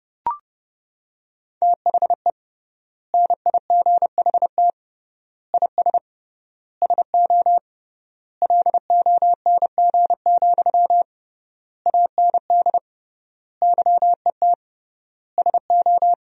Morse code, heard as a keyed tone, about 30 wpm, 700 Hz. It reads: THE NIGHT IS SO LONG, AND YET SO
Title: 3x Word-Spacing - Sentences from Top 500 Words - 30wpm